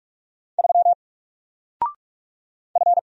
Morse code, heard as a keyed tone, about 44 wpm, 700 Hz. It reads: 3 F
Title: Single Letter-Number 45wpm